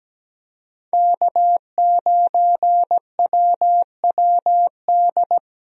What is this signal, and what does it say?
Morse code, heard as a keyed tone, about 17 wpm, 700 Hz. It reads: K9WWD
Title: USA Call Signs - 5 Characters - Repeated 3x - 17wpm